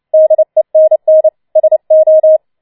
Morse code, heard as a keyed tone, about 29 wpm, 630 Hz. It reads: DENN SO